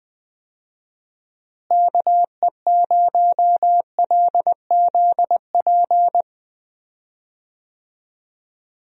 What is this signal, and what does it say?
Morse code, heard as a keyed tone, about 20 wpm, 700 Hz. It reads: KE0LZP
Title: USA Call Signs - 6 Characters - Repeated 3x - 20wpm